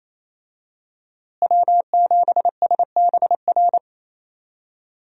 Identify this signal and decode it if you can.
Morse code, heard as a keyed tone, about 28 wpm, 700 Hz. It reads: W7SBR
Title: USA Call Signs - 5 Characters - Repeated 3x - 28wpm